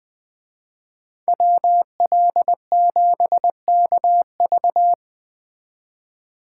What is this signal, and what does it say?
Morse code, heard as a keyed tone, about 20 wpm, 700 Hz. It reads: WL7KV
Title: Speed Racing - USA 2x2 Call Signs 17wpm